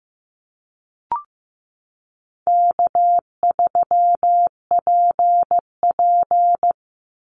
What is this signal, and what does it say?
Morse code, heard as a keyed tone, about 15 wpm, 700 Hz. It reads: K3PP
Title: Easy Call Signs - 15wpm